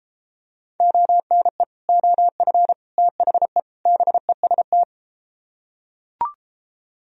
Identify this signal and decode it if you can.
Morse code, heard as a keyed tone, about 32 wpm, 700 Hz. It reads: ONE OF THE BEST